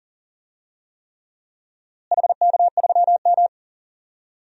Morse code, heard as a keyed tone, about 40 wpm, 700 Hz. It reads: HK3M
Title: DXCC-100 Call Signs - 4 Characters - Repeated 3x - 40wpm